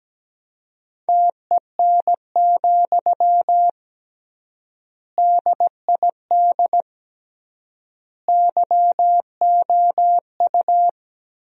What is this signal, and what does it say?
Morse code, heard as a keyed tone, about 17 wpm, 700 Hz. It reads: TEN, DID YOU
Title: 3x Word-Spacing - Sentences from Top 400 Words - 17wpm